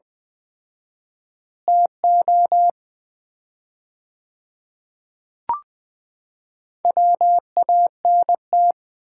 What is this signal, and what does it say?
Morse code, heard as a keyed tone, about 20 wpm, 700 Hz. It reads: TO WANT
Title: 4x Word-Spacing - Sets of 5 Words - 20wpm